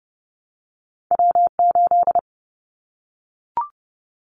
Morse code, heard as a keyed tone, about 30 wpm, 700 Hz. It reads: W8
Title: Callsign Prefixes for Top 25 DXCC - 30wpm